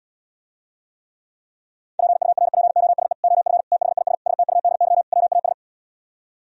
Morse code, heard as a keyed tone, about 77 wpm, 700 Hz. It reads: PURPOSE OF THIS STUDY WAS